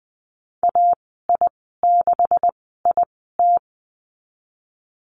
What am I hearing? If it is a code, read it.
Morse code, heard as a keyed tone, about 20 wpm, spaced as for 15 wpm, 700 Hz. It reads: AI6IT
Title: 193 Lesson 31g - Letter v - Call Signs 20wpm